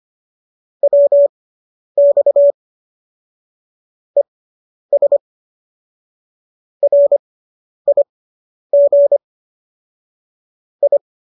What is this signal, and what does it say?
Morse code, heard as a keyed tone, about 25 wpm, spaced as for 10 wpm, 570 Hz. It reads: WX ES RIG I